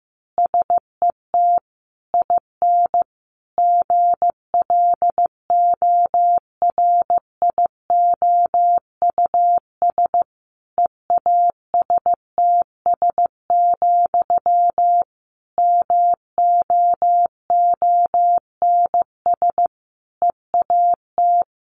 Morse code, heard as a keyed tone, about 15 wpm, 700 Hz. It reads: SET IN GLORIOUS EASTS, MOONS EAT
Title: A Sound Of Thunder 15wpm